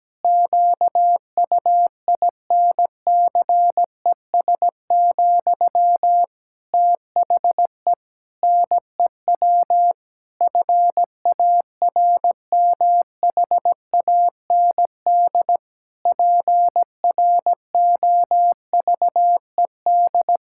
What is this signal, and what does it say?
Morse code, heard as a keyed tone, about 17 wpm, 700 Hz. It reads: QUINCES, THE NEW FARMHAND PROVED
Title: Pangram Challenge v2 - 17wpm